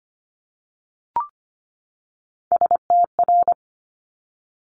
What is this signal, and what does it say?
Morse code, heard as a keyed tone, about 24 wpm, 700 Hz. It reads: STR
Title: Multi-letter Phonemes 25wpm